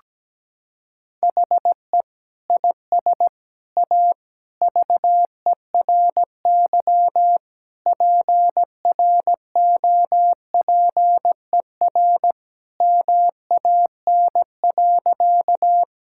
Morse code, heard as a keyed tone, about 17 wpm, 700 Hz. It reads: HE IS A VERY PROPER MAN.
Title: Sentences from Top 1000 Words - Review Part 5 - 17wpm